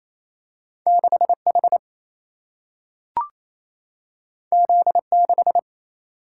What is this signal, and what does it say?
Morse code, heard as a keyed tone, about 28 wpm, 700 Hz. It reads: 6 H Z 6
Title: Single-Letter Single-Number 28wpm